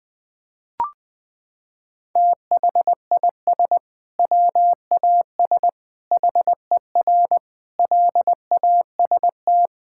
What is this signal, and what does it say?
Morse code, heard as a keyed tone, about 20 wpm, 700 Hz. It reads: THIS WAS HER LAST
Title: Sentences from Top 1000 Words - Review Part 6 - 20wpm